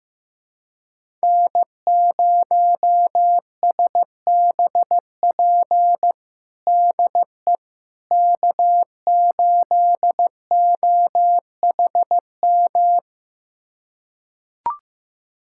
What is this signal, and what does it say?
Morse code, heard as a keyed tone, about 15 wpm, 700 Hz. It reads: N0SBP DE K8OHM
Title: USA Callsign DE Callsign - 5 Characters - 15wpm